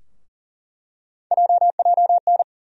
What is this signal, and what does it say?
Morse code, heard as a keyed tone, about 40 wpm, 700 Hz. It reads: JJN
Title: USA Call Sign Postfixes - 3 Letters 40wpm